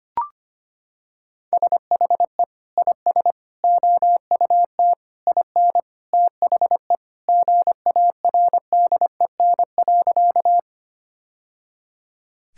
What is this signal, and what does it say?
Morse code, heard as a keyed tone, about 25 wpm, 700 Hz. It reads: SHE IS OUT IN THE GARDEN.
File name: practice / 25wpm / Sentences with 3-Word Phrases - 25wpm